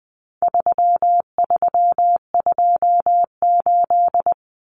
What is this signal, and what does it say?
Morse code, heard as a keyed tone, about 20 wpm, 700 Hz. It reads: 3328
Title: Common Cut Numbers - 4-Digit 20wpm